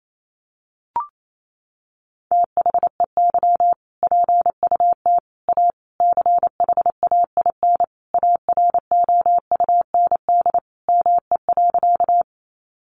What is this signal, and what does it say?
Morse code, heard as a keyed tone, about 28 wpm, 700 Hz. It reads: THEY PUT A CHAIN AROUND ME.
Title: Sentences from Top 1500 Words 28wpm